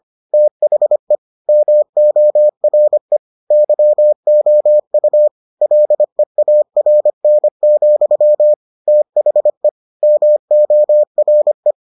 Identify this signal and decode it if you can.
Morse code, heard as a keyed tone, about 25 wpm, 600 Hz. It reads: THE MORE YOU LEARN, THE MORE